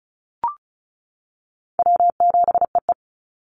Morse code, heard as a keyed tone, about 34 wpm, 700 Hz. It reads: W7EE
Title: Easy Call Signs Repeated 2x - 35wpm